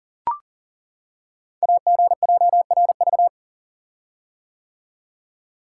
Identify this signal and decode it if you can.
Morse code, heard as a keyed tone, about 40 wpm, 700 Hz. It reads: AGJRV
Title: Code Groups - 5 Letter:Numbers - 40wpm